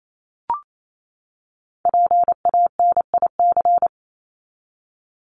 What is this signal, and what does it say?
Morse code, heard as a keyed tone, about 28 wpm, 700 Hz. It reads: PANIC
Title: Top 1751-2000 Words 28wpm